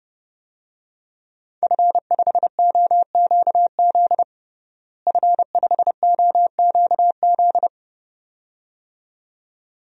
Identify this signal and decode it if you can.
Morse code, heard as a keyed tone, about 30 wpm, 700 Hz. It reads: F5OQZ F5OQZ
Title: DXCC-100 Call Signs - 5 Characters - Repeated 2x - 30wpm